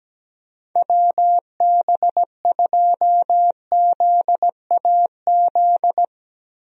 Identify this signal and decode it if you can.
Morse code, heard as a keyed tone, about 17 wpm, 700 Hz. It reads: WB2ZAZ
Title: USA Call Signs - 6 Characters - Repeated 3x - 17wpm